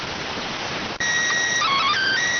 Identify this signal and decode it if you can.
SSTV (slow-scan television), PD 290.